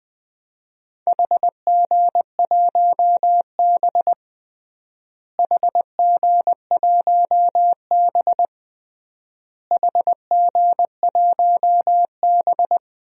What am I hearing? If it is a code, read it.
Morse code, heard as a keyed tone, about 21 wpm, 700 Hz. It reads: HG1B HG1B HG1B
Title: DXCC-100 Call Signs - 4 Characters - Repeated 3x - 20wpm